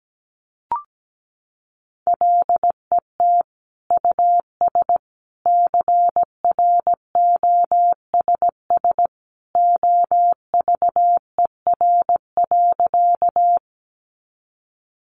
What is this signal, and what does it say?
Morse code, heard as a keyed tone, about 17 wpm, 700 Hz. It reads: LET US CROSS OVER.
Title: Sentences from Top 300 Words 17wpm